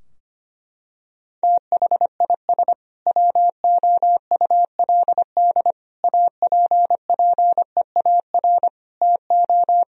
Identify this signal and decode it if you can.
Morse code, heard as a keyed tone, about 25 wpm, 700 Hz. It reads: THIS WOULD APPEAR TO
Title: Sentences from Top 1000 Words - Review Part 3 - 25wpm